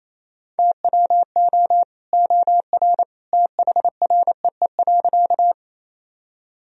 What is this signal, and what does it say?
Morse code, heard as a keyed tone, about 28 wpm, 700 Hz. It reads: TWO OR THREE.
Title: Sentences from Top 1000 Words - Review Part 1 - 28wpm